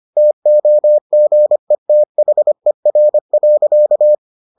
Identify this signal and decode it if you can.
Morse code, heard as a keyed tone, about 25 wpm, 600 Hz. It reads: TOGETHER.